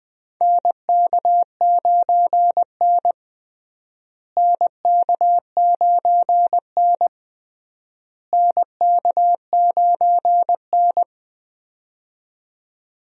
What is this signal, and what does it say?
Morse code, heard as a keyed tone, about 21 wpm, 700 Hz. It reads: NK9N NK9N NK9N
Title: USA Call Signs - 4 Characters - Repeated 3x - 20wpm